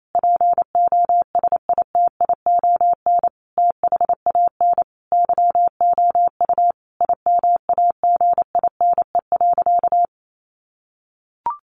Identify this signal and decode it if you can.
Morse code, heard as a keyed tone, about 28 wpm, 700 Hz. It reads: POSITION THAN YOU IMAGINE.
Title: Sentences from Top 1400 Words 28wpm